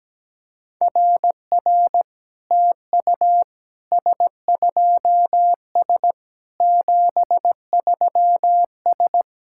Morse code, heard as a keyed tone, about 17 wpm, 700 Hz. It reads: RR TU S2S 73S
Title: SOTA QSO Practice 17wpm